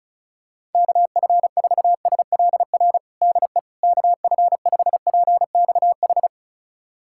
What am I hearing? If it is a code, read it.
Morse code, heard as a keyed tone, about 34 wpm, 700 Hz. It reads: KF4SLR DE KF5PXH
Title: USA Callsign DE Callsign - 6 Characters - 35wpm